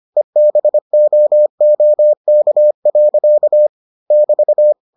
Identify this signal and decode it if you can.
Morse code, heard as a keyed tone, about 25 wpm, 600 Hz. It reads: EBOOK. <BT>